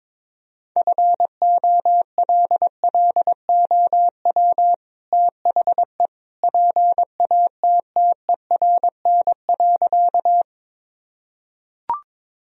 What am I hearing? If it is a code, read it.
Morse code, heard as a keyed tone, about 22 wpm, 700 Hz. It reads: FOLLOW THE PATTERN.
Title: Sentences from Top 1000 Words - Review Part 2 - 22wpm